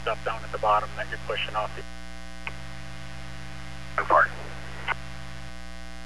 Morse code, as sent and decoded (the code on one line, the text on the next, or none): none